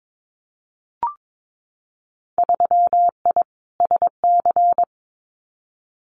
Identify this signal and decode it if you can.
Morse code, heard as a keyed tone, about 22 wpm, 700 Hz. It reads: 3I SC
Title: Winter Field Day Exchanges 22wpm